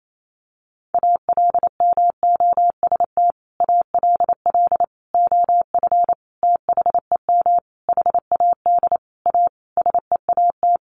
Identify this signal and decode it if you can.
Morse code, heard as a keyed tone, about 28 wpm, 700 Hz. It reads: ALMOST ALL OF THEM HAD A SEAT
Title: Sentences from Top 1500 Words 28wpm